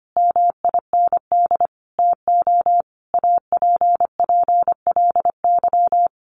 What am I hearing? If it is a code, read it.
Morse code, heard as a keyed tone, about 25 wpm, 700 Hz. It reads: MIND TO APPLY
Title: Sentences from Top 1400 Words 25wpm